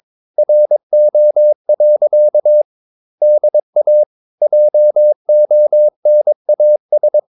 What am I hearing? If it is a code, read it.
Morse code, heard as a keyed tone, about 22 wpm, 600 Hz. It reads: RO. DA JONAS
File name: Ferie QRS 22wpm